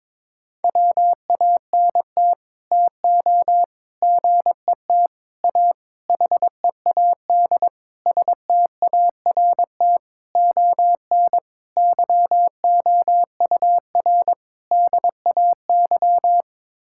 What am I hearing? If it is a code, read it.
Morse code, heard as a keyed tone, about 22 wpm, 700 Hz. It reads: WANT TO GET A HEAD START ON YOUR DAY
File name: Sets of 9 Words - 22wpm